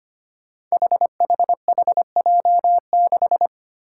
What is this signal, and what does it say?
Morse code, heard as a keyed tone, about 25 wpm, 700 Hz. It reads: HHHJ6
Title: Five Challenging Letter-Number Combinations 25wpm